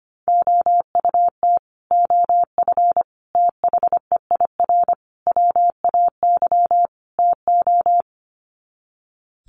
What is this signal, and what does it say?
Morse code, heard as a keyed tone, about 25 wpm, 700 Hz. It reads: OUT OF THEIR WAY TO
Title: Sets of 5 Words Encore2 25wpm